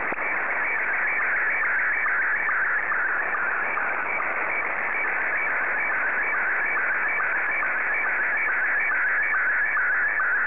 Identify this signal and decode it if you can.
SSTV (slow-scan television), Scottie 1.
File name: sstv / S1-VK7NMH